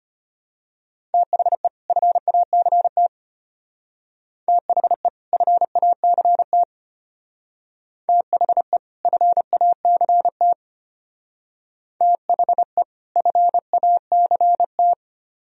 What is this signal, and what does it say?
Morse code, heard as a keyed tone, about 30 wpm, 700 Hz. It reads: THE FACT THE FACT THE FACT THE FACT
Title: Speed-Racing - Sets of 2 Words 25wpm